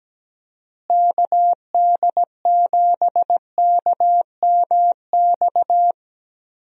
Morse code, heard as a keyed tone, about 17 wpm, 700 Hz. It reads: KD7KMX
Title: USA Call Signs - 6 Characters - Repeated 2x - 17wpm